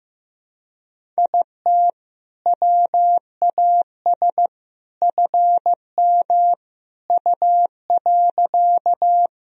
Morse code, heard as a keyed tone, about 15 wpm, 700 Hz. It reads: IT WAS FM U.
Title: Sentences from Top 100 Words and Abbreviations 15wpm